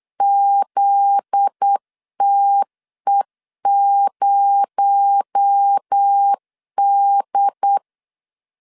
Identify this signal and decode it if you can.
Morse code, heard as a keyed tone, about 8 wpm, 800 Hz. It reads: ZTE0D
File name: CW / DET / 1 9 1500C S C 8wpm 5653